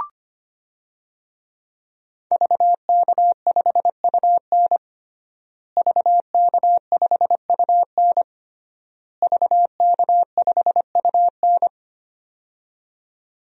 Morse code, heard as a keyed tone, about 25 wpm, 700 Hz. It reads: VK5UN VK5UN VK5UN
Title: DXCC-100 Call Signs - 5 Characters - Repeated 3x - 25wpm